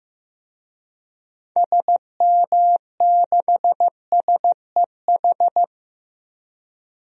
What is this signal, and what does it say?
Morse code, heard as a keyed tone, about 15 wpm, 700 Hz. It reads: SM6SEH
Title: DXCC-100 Call Signs - 6 Characters - Repeated 2x - 15wpm